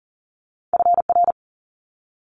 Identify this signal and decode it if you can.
Morse code, heard as a keyed tone, about 42 wpm, 700 Hz. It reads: FR
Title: Multi-letter Phonemes Quick Fire 40wpm